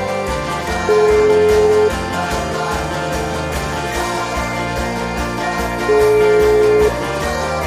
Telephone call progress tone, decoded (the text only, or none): ringback tone